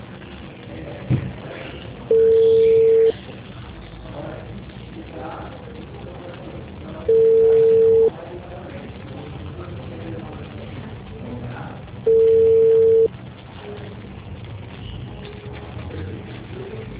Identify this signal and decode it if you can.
Ringback tone (China, 450 Hz), with speech.